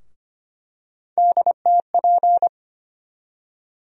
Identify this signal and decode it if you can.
Morse code, heard as a keyed tone, about 25 wpm, 700 Hz. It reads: DTP
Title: USA Call Sign Postfixes - 3 Letters 25wpm